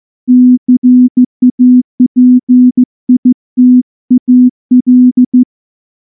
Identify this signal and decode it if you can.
Morse code, heard as a keyed tone, about 14 wpm, 250 Hz. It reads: CAPITAL
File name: morse / morse-training-15WPM